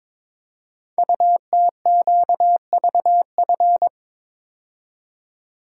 Morse code, heard as a keyed tone, about 22 wpm, 700 Hz. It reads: UTQVF
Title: Code Groups - 5 Letters - 22wpm